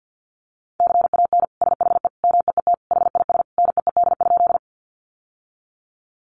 Morse code, heard as a keyed tone, about 49 wpm, 700 Hz. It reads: CAN SHE MEET HIS NEED?
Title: Sentences from Top 1100 Words 50wpm